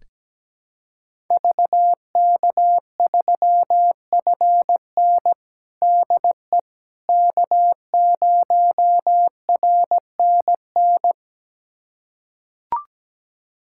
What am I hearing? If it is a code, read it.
Morse code, heard as a keyed tone, about 17 wpm, 700 Hz. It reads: VK3FN DE K0RNN
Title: USA Callsign DE Callsign - 5 Characters - 17wpm